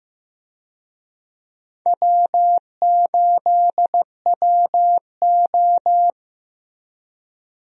Morse code, heard as a keyed tone, about 16 wpm, 700 Hz. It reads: W8WO
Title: USA Call Signs - 4 Characters - Repeated 2x - 15wpm